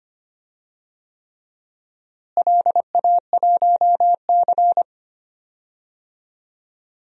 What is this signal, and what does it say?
Morse code, heard as a keyed tone, about 25 wpm, 700 Hz. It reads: LA1C
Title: DXCC-100 Call Signs - 4 Characters - Repeated 3x - 25wpm